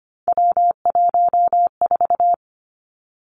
Morse code, heard as a keyed tone, about 25 wpm, 700 Hz. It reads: W14
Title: Three Challenging Letter-Number Combinations 25wpm